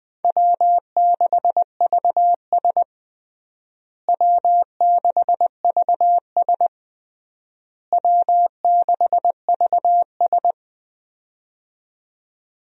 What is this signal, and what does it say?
Morse code, heard as a keyed tone, about 21 wpm, 700 Hz. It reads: W6VS W6VS W6VS
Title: USA Call Signs - 4 Characters - Repeated 3x - 20wpm